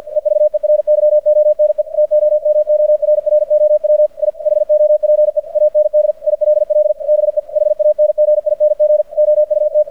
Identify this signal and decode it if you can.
Morse code, heard as a keyed tone, about 50 wpm, 600 Hz. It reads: EVERYONE TOMORROW I HOPE AND ILL BE HANGING OUT
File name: archived / CRSnet